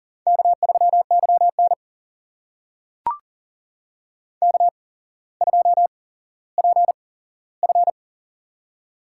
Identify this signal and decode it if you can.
Morse code, heard as a keyed tone, about 40 wpm, spaced as for 17 wpm, 700 Hz. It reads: K3YN K2PF
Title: USA Call Signs - 4 Characters - 8x Character Spacing - 40wpm